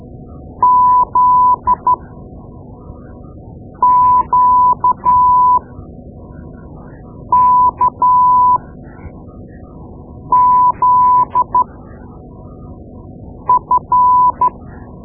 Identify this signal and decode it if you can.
Morse code, heard as a keyed tone, about 12 wpm, spaced as for 4 wpm, 1000 Hz. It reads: ZQKZF
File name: Morse Lessons / Morse Lessons / Tape-2B